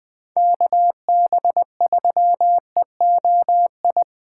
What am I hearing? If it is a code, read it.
Morse code, heard as a keyed tone, about 20 wpm, 700 Hz. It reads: KB3EOI
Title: USA Call Signs - 6 Characters - 3x Character Spacing - 20wpm